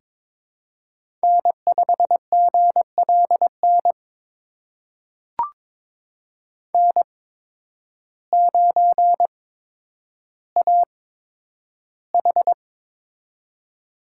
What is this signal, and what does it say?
Morse code, heard as a keyed tone, about 22 wpm, spaced as for 10 wpm, 700 Hz. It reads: N5GLN N9AH
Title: USA Call Signs - 8x Character Spacing - 22wpm